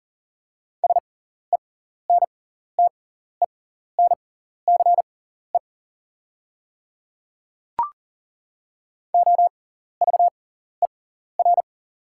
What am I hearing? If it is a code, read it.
Morse code, heard as a keyed tone, about 40 wpm, spaced as for 14 wpm, 700 Hz. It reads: SENTENCE OVER
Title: 6x Character Spacing - Top 200 Words - 40wpm